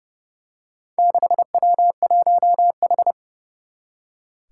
Morse code, heard as a keyed tone, about 32 wpm, 700 Hz. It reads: 6W1H
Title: Four Challenging Letter-Number Combinations 30wpm